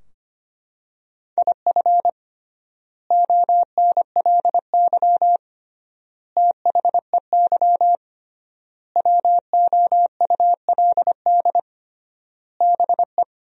Morse code, heard as a keyed tone, about 25 wpm, 700 Hz. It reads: IF ONLY THEY WOULD BE
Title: 3x Word-Spacing - Sentences from Top 500 Words - 25wpm